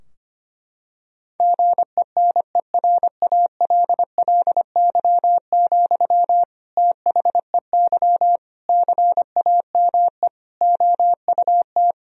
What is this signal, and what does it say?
Morse code, heard as a keyed tone, about 25 wpm, 700 Hz. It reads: GENERALLY, THEY CAME OUT
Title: Sentences from Top 1000 Words - Review Part 4 - 25wpm